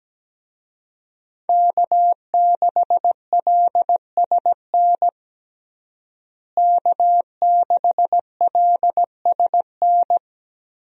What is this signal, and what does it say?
Morse code, heard as a keyed tone, about 17 wpm, 700 Hz. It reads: K6LSN K6LSN
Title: USA Call Signs - 5 Characters - Repeated 2x - 17wpm